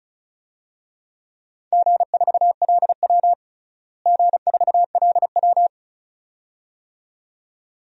Morse code, heard as a keyed tone, about 36 wpm, 700 Hz. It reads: G4LW G4LW
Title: DXCC-100 Call Signs - 4 Characters - Repeated 2x - 35wpm